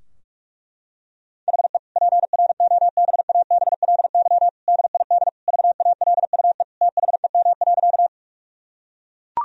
Morse code, heard as a keyed tone, about 41 wpm, 700 Hz. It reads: HE PROBABLY DID VALUE THEM.
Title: Sentences from Top 1000 Words - Review Part 4 - 45wpm